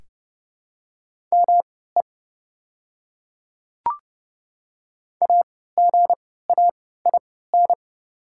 Morse code, heard as a keyed tone, about 30 wpm, spaced as for 17 wpm, 690 Hz. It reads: ME AGAIN
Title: 3x Character Spacing - Top 200 Words - 30wpm